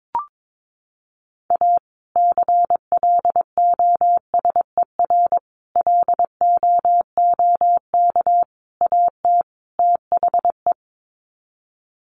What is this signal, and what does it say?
Morse code, heard as a keyed tone, about 22 wpm, 700 Hz. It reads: A CLOSER LOOK AT THE
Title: Sets of 5 Words Encore 22wpm